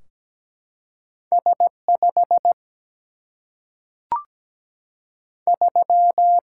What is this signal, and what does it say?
Morse code, heard as a keyed tone, about 17 wpm, 700 Hz. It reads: S 5 3
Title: Single-Letter Single-Number 17wpm